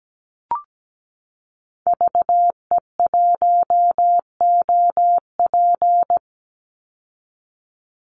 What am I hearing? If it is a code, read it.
Morse code, heard as a keyed tone, about 17 wpm, 700 Hz. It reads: VE1OP
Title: Call Signs Repeated 2x - 17wpm